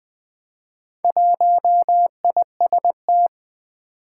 Morse code, heard as a keyed tone, about 21 wpm, 700 Hz. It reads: 1IST
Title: Four Challenging Letter-Number Combinations 20wpm